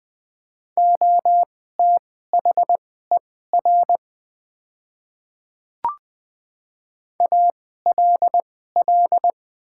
Morse code, heard as a keyed tone, about 20 wpm, spaced as for 14 wpm, 700 Hz. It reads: OTHER ALL
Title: 2x Character Spacing - Top 100 Words - 20wpm